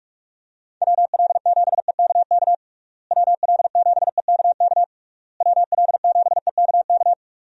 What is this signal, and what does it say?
Morse code, heard as a keyed tone, about 46 wpm, 700 Hz. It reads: WL7EKK WL7EKK WL7EKK
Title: USA Call Signs - 6 Characters - Repeated 3x - 45wpm